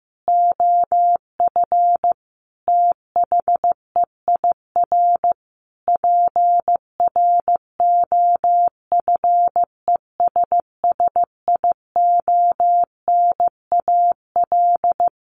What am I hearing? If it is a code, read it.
Morse code, heard as a keyed tone, about 15 wpm, 700 Hz. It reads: OF THEIR PROFESSIONAL
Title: Sentences from Top 1400 Words 15wpm